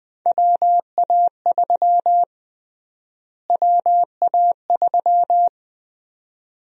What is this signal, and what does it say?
Morse code, heard as a keyed tone, about 21 wpm, 700 Hz. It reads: WA3 WA3
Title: USA Call Sign Prefixes - Repeated 2x - 20wpm